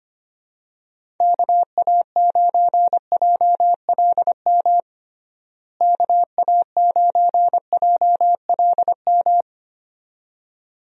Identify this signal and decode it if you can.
Morse code, heard as a keyed tone, about 25 wpm, 700 Hz. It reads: KA9JLM KA9JLM
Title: USA Call Signs - 6 Characters - Repeated 2x - 25wpm